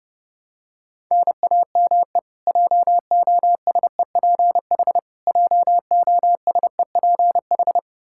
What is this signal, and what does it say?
Morse code, heard as a keyed tone, about 30 wpm, 700 Hz. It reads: NAME JOSEPH JOSEPH
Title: K1USN SST QSO Practice 30wpm